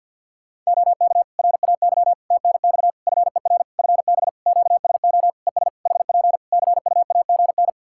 Morse code, heard as a keyed tone, about 48 wpm, 700 Hz. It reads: OK RAY TNX FER FB QSO ES HP CUAGN
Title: Practice QSO - Exchange 4 - Ending - 50wpm